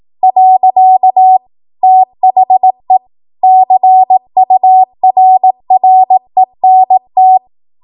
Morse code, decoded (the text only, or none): . THE CURRENT